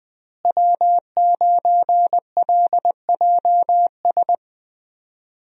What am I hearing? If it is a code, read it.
Morse code, heard as a keyed tone, about 20 wpm, 700 Hz. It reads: W9LJS
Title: USA Call Signs - 5 Characters - Repeated 3x - 20wpm